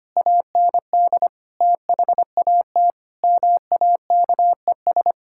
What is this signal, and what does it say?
Morse code, heard as a keyed tone, about 25 wpm, 700 Hz. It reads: AND THAT MAKES
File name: practice / 25wpm / Sentences with 2-Word Phrases - 25wpm